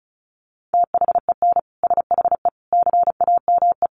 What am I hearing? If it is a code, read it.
Morse code, heard as a keyed tone, about 34 wpm, 700 Hz. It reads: THEN SHE CAME
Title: Sentences from Top 1300 Words 35wpm